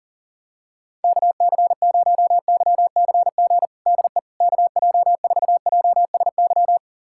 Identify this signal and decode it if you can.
Morse code, heard as a keyed tone, about 37 wpm, 700 Hz. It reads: KC0YCG DE KJ4JSY
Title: USA Callsign DE Callsign - 6 Characters - 40wpm